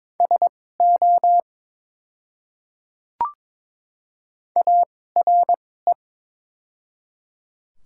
Morse code, heard as a keyed tone, about 22 wpm, spaced as for 16 wpm, 700 Hz. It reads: SO ARE
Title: 2x Character Spacing - Top 300 Words - 22wpm